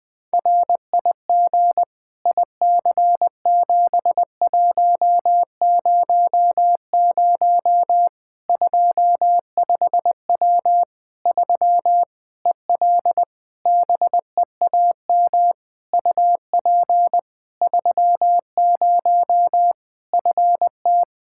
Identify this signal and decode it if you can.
Morse code, heard as a keyed tone, about 21 wpm, 700 Hz. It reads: RIG IC7100 25W 3 EL BEAM UP 30 FT
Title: Practice QSO - Exchange 2 - Info - 20wpm